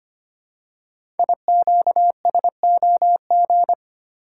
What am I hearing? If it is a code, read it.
Morse code, heard as a keyed tone, about 25 wpm, 700 Hz. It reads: IQSOG
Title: Code Groups - 5 Letters - 25wpm